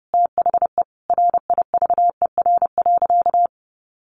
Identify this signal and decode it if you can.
Morse code, heard as a keyed tone, about 30 wpm, 700 Hz. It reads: THE RIVER.
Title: Sentences from Top 1200 Words 30wpm